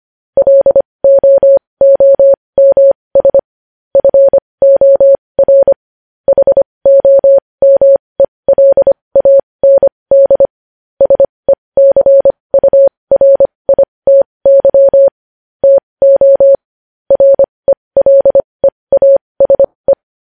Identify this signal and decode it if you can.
Morse code, heard as a keyed tone, about 25 wpm, spaced as for 20 wpm, 550 Hz. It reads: LOOMS FOR HOMELAND SECURITY TO RELEASE
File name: podcast / News headlines-1x-25x20